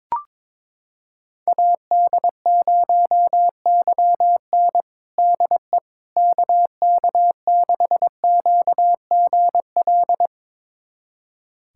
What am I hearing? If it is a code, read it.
Morse code, heard as a keyed tone, about 22 wpm, 700 Hz. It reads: AD0YN DE KK6QGL
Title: USA Callsign DE Callsign 22wpm